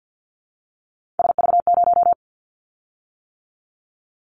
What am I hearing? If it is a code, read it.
Morse code, heard as a keyed tone, about 50 wpm, 700 Hz. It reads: SV0
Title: Callsign Prefixes for Top 50 DXCC - 50wpm